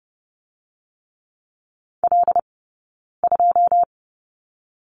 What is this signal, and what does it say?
Morse code, heard as a keyed tone, about 30 wpm, 700 Hz. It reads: L 2
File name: Lesson 15d - Number 5 - Two Characters 30wpm